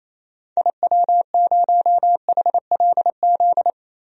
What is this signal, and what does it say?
Morse code, heard as a keyed tone, about 28 wpm, 700 Hz. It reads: IW0HLZ
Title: DXCC-100 Call Signs - 6 Characters - 28wpm